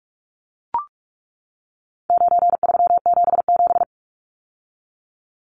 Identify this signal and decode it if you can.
Morse code, heard as a keyed tone, about 44 wpm, 700 Hz. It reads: 9377
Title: Numbers - 4-Digit 45wpm